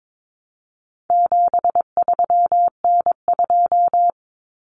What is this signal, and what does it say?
Morse code, heard as a keyed tone, about 22 wpm, 700 Hz. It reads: 73N2
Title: Common Cut Numbers - 4-Digit 22wpm